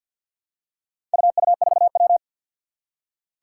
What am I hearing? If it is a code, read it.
Morse code, heard as a keyed tone, about 47 wpm, 700 Hz. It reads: UU4W
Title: Four Challenging Letter-Number Combinations 50wpm